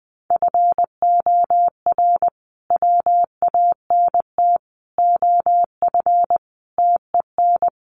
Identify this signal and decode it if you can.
Morse code, heard as a keyed tone, about 20 wpm, 700 Hz. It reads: FOR WANT OF TEN
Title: A Sound Of Thunder 20wpm